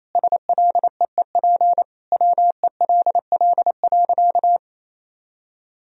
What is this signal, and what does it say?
Morse code, heard as a keyed tone, about 28 wpm, 700 Hz. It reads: SLEEP WELL.
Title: Sentences from Top 500 Words and Abbreviations 28wpm